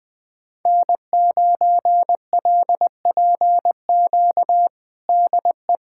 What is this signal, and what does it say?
Morse code, heard as a keyed tone, about 20 wpm, 700 Hz. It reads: N9LPQ DE
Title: USA Callsign DE Callsign 20wpm